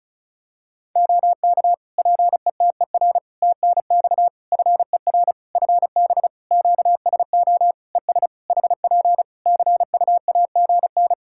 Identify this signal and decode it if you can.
Morse code, heard as a keyed tone, about 36 wpm, 700 Hz. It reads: OK PETER TNX FER FB QSO ES HP CUAGN
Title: K1USN SST QSO Practice 35wpm